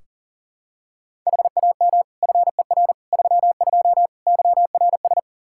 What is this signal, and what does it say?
Morse code, heard as a keyed tone, about 37 wpm, 700 Hz. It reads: HAM FER 32 YRS
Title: Practice QSO - Exchange 3 - More Info - 40wpm